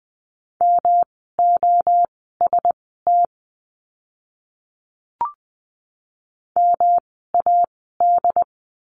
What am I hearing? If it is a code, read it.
Morse code, heard as a keyed tone, about 20 wpm, spaced as for 14 wpm, 700 Hz. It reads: MOST MAD
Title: 108 Lesson 18f - Letter m - Words 20wpm